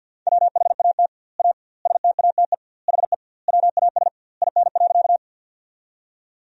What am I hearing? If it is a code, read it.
Morse code, heard as a keyed tone, about 50 wpm, 700 Hz. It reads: WHAT A STATE HE WAS IN.
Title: 2x Word-Spacing - Sentences from Top 400 Words - 50wpm